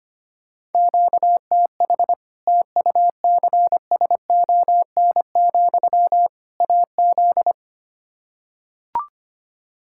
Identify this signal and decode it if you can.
Morse code, heard as a keyed tone, about 25 wpm, 700 Hz. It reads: QTH TUCSON, AZ
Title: Practice QSO - Exchange 1 - Beginning - 25wpm